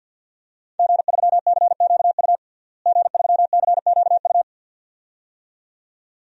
Morse code, heard as a keyed tone, about 50 wpm, 700 Hz. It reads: G3CQU G3CQU
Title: DXCC-100 Call Signs - 5 Characters - Repeated 2x - 50wpm